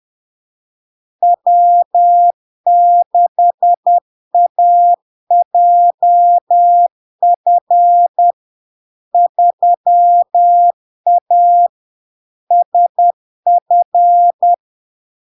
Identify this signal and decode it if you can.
Morse code, heard as a keyed tone, about 10 wpm, 700 Hz. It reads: W6AJF 3A SF
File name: LARK-FD-2022-Log-in-CW-10wpm-700-Hz